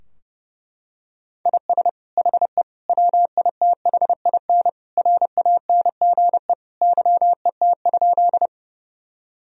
Morse code, heard as a keyed tone, about 32 wpm, 700 Hz. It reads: IS HE WITHIN RANGE YET?
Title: Sentences from Top 1000 Words - Review Part 4 - 30wpm